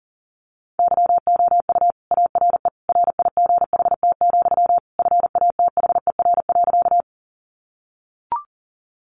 Morse code, heard as a keyed tone, about 40 wpm, 700 Hz. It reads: YOU ARE RIGHT, FATHER.
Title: Sentences from Top 300 Words 40wpm